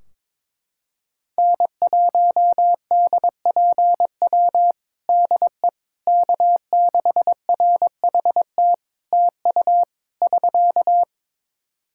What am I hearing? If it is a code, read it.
Morse code, heard as a keyed tone, about 22 wpm, 700 Hz. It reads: N1DPW DE K6RHT TU <SK>
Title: Practice QSO - Exchange 4 - Ending - 22wpm